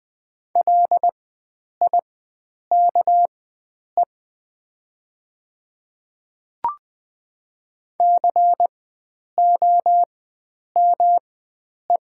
Morse code, heard as a keyed tone, about 20 wpm, spaced as for 9 wpm, 700 Hz. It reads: LIKE COME
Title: 4x Character Spacing - Top 100 Words - 20wpm